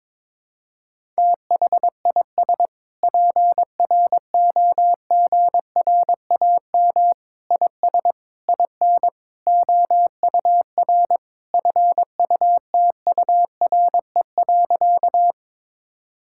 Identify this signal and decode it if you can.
Morse code, heard as a keyed tone, about 22 wpm, 700 Hz. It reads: THIS PROGRAM IS IN OUR FUTURE.
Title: Sentences from Top 1000 Words - Review Part 5 - 22wpm